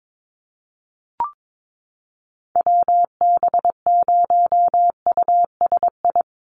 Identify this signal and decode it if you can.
Morse code, heard as a keyed tone, about 22 wpm, 700 Hz. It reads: WB0USI
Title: NA SOTA Activator Callsigns 22wpm